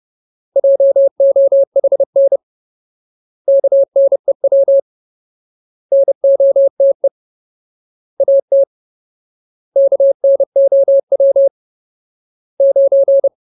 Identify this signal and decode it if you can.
Morse code, heard as a keyed tone, about 30 wpm, 550 Hz. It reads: JOHN KNEW NOTE AT KNOW 9
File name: Ultimate Word List / 30WPM Ultimate